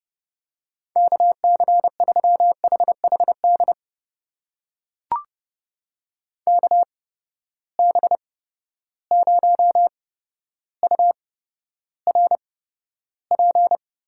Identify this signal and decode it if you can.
Morse code, heard as a keyed tone, about 30 wpm, spaced as for 13 wpm, 700 Hz. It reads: KC3HHD KB0URP
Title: USA Call Signs - 6 Characters - 8x Character Spacing - 30wpm